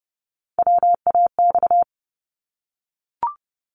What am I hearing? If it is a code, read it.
Morse code, heard as a keyed tone, about 30 wpm, 700 Hz. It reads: WAX
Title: Multi-letter Phonemes 30wpm